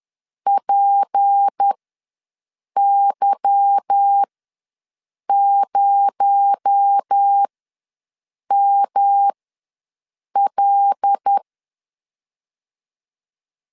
Morse code, heard as a keyed tone, about 10 wpm, spaced as for 6 wpm, 800 Hz. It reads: PY0ML